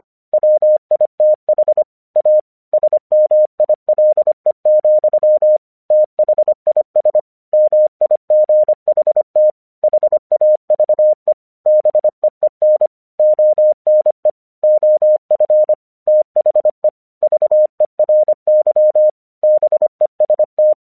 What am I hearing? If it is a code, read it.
Morse code, heard as a keyed tone, about 25 wpm, 610 Hz. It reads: WITH A SMILE, THIS MIGHT HAVE BEEN ONE OF THE VERY BEST
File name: 1 English 25wpm / AQC36